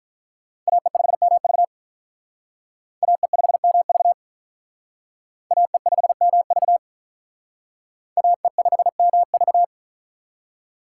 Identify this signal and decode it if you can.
Morse code, heard as a keyed tone, about 44 wpm, 700 Hz. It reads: AE5MV AE5MV AE5MV AE5MV